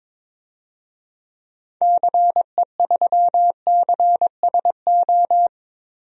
Morse code, heard as a keyed tone, about 22 wpm, 700 Hz. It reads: CE3CSO
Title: DXCC-100 Call Signs - 6 Characters - 22wpm